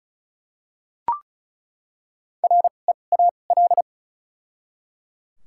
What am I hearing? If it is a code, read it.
Morse code, heard as a keyed tone, about 36 wpm, spaced as for 25 wpm, 700 Hz. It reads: REAL
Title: 2x Character Spacing - Top 500 Words - 35wpm